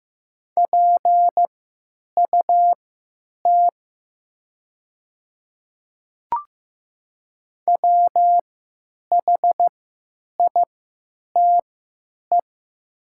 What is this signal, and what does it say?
Morse code, heard as a keyed tone, about 15 wpm, spaced as for 9 wpm, 700 Hz. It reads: PUT WHITE
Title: 3x Character Spacing - Top 500 Words - 15wpm